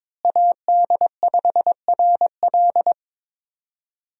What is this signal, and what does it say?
Morse code, heard as a keyed tone, about 22 wpm, 700 Hz. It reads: AD5RL
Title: USA Call Signs - 5 Characters - Repeated 3x - 22wpm